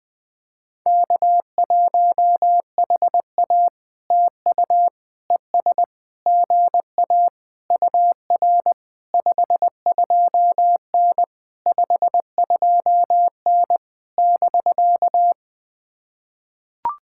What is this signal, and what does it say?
Morse code, heard as a keyed tone, about 20 wpm, 700 Hz. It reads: K1HA TU ES GA UR 52N 52N <BK>
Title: POTA QSO Practice - 20wpm